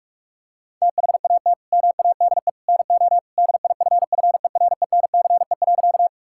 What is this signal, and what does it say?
Morse code, heard as a keyed tone, about 46 wpm, 690 Hz. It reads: THAT MADE NO DIFFERENCE.
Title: Sentences from Top 1000 Words - Review Part 5 - 45wpm